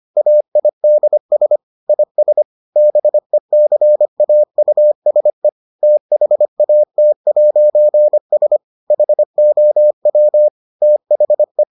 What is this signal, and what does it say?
Morse code, heard as a keyed tone, about 25 wpm, 600 Hz. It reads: AIDS IS BECAUSE THAT'S HOW THE